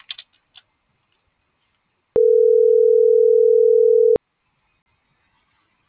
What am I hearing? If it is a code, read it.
Ringback tone (North America), 440 + 480 Hz.